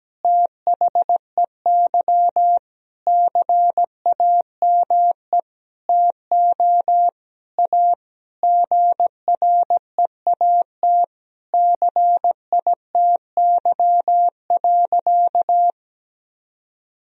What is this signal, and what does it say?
Morse code, heard as a keyed tone, about 17 wpm, 700 Hz. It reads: THEY CAME TO A GREAT CITY.
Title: Sentences from Top 1000 Words - Review Part 1 - 17wpm